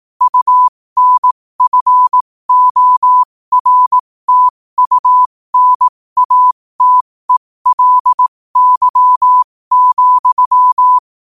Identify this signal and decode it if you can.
Morse code, heard as a keyed tone, about 18 wpm, spaced as for 15 wpm, 1000 Hz. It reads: UNFORTUNATELY,
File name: qotdfiles / 15wpm